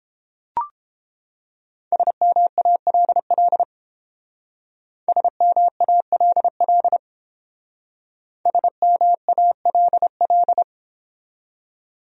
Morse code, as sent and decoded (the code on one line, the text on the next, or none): ... -- .- .-.. .-.. / ... -- .- .-.. .-.. / ... -- .- .-.. .-..
SMALL SMALL SMALL